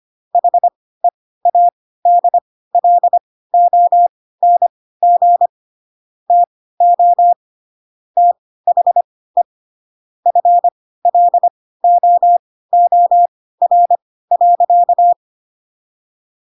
Morse code, heard as a keyed tone, about 25 wpm, spaced as for 16 wpm, 700 Hz. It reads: HEADLONG TO THE FLOOR.